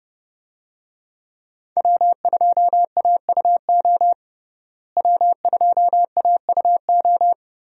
Morse code, heard as a keyed tone, about 28 wpm, 700 Hz. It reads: W2AUO W2AUO
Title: USA Call Signs - 5 Characters - Repeated 3x - 30wpm